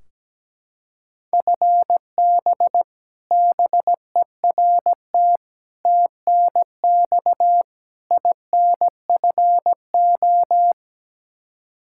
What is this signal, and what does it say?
Morse code, heard as a keyed tone, about 17 wpm, 700 Hz. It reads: FB BERT TNX INFO
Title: Practice QSO - Exchange 3 - More Info - 17wpm